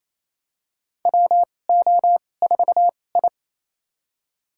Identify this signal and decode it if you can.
Morse code, heard as a keyed tone, about 28 wpm, spaced as for 20 wpm, 700 Hz. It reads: WO4I
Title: USA Call Signs - 4 Characters - 2x Character Spacing - 28wpm